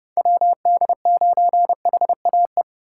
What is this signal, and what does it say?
Morse code, heard as a keyed tone, about 29 wpm, 700 Hz. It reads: WD9HAE